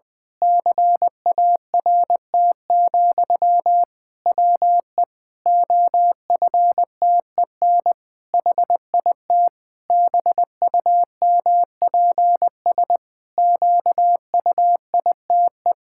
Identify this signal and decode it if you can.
Morse code, heard as a keyed tone, about 19 wpm, 700 Hz. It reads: CART, WE OFTEN HIT BUMPS QUITE
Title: Pangram Challenge v2 - 20wpm